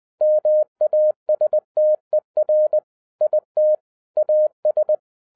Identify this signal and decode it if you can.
Morse code, heard as a keyed tone, about 20 wpm, 610 Hz. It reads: MASTER IT AS
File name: 2012-01-mp3 / n05-CAME BACK TO HAM RADIO AFTER 30 YEARS OF INTERVAL-20wpm